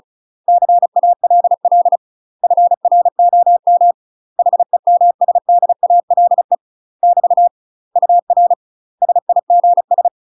Morse code, heard as a keyed tone, about 36 wpm, 700 Hz. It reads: CALL FROM HEMSDALE <BT> UR SIGS
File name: UK MASTER QSOs TRAINING / 35WPM / UK MASTER QSOs 0035